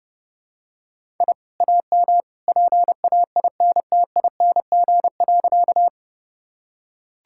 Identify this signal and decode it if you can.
Morse code, heard as a keyed tone, about 30 wpm, 700 Hz. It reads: I AM PAINTING.
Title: Sentences from Top 1000 Words - Review Part 6 - 30wpm